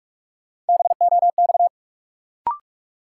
Morse code, heard as a keyed tone, about 42 wpm, 700 Hz. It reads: BOX
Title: Sentences from Top 500 Words - New Word Drill - 45wpm